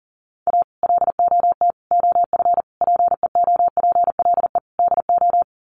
Morse code, heard as a keyed tone, about 37 wpm, 700 Hz. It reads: A LOT OF PEOPLE DO
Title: Sets of 5 Words 40wpm